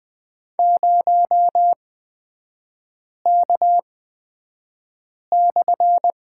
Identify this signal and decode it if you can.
Morse code, heard as a keyed tone, about 20 wpm, spaced as for 5 wpm, 690 Hz. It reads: 0K/
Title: 229 Lesson 37e - Letter x - Three Characters 20wpm